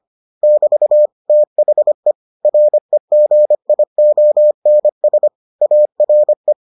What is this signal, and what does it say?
Morse code, heard as a keyed tone, about 25 wpm, 600 Hz. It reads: <BT> THE REGIONS ARE